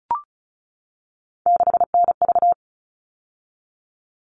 Morse code, heard as a keyed tone, about 34 wpm, 700 Hz. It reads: 6NV
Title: All Cut Numbers - 3-Digit 35wpm